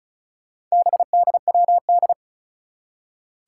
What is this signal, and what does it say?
Morse code, heard as a keyed tone, about 36 wpm, 700 Hz. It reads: BDWD
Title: Four Challenging Letter-Number Combinations 35wpm